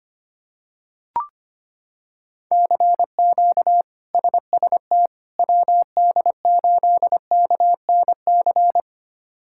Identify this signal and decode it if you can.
Morse code, heard as a keyed tone, about 25 wpm, 700 Hz. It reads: CQ SST WD8KNC
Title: K1USN SST QSO Practice - 25wpm